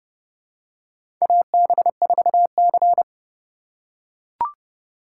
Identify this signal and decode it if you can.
Morse code, heard as a keyed tone, about 31 wpm, 700 Hz. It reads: AB4C
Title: USA Call Signs - 4 Characters - 4x Character Spacing - 30wpm